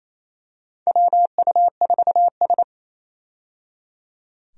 Morse code, heard as a keyed tone, about 27 wpm, 700 Hz. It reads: WU4S
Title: USA Call Signs - 4 Characters - 28wpm